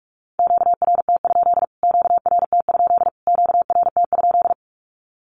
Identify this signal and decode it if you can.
Morse code, heard as a keyed tone, about 44 wpm, 700 Hz. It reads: QRT? QRT? QRT?
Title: Q-Codes Repeated 3x 45wpm